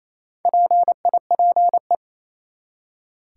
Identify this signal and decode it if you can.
Morse code, heard as a keyed tone, about 28 wpm, 700 Hz. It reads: PIPE
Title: Four Letter Words - 28wpm